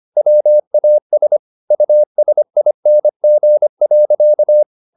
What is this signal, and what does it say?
Morse code, heard as a keyed tone, about 25 wpm, 600 Hz. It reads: WAS USING.
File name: the old man and the sea part 1 25wpm